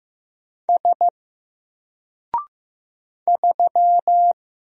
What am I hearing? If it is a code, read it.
Morse code, heard as a keyed tone, about 15 wpm, 700 Hz. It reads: S 3
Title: Single Character (Letter:Number:Punctuation:Prosign) Alt 15wpm